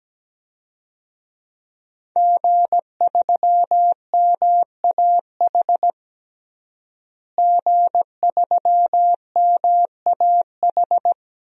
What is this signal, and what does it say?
Morse code, heard as a keyed tone, about 17 wpm, 700 Hz. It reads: G3MAH G3MAH
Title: DXCC-100 Call Signs - 5 Characters - Repeated 2x - 17wpm